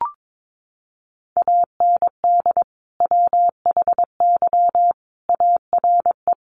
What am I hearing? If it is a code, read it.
Morse code, heard as a keyed tone, about 22 wpm, 700 Hz. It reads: AND WHY ARE
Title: Sentences from Top 1000 Words 22wpm